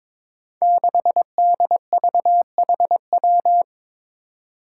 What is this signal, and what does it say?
Morse code, heard as a keyed tone, about 22 wpm, 700 Hz. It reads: 6DVHW
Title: Five Challenging Letter-Number Combinations 22wpm